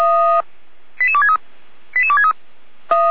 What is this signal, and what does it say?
Five-tone selective call (ZVEI), heard as quick runs of five tones, tones of about 70 ms. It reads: 88263 88263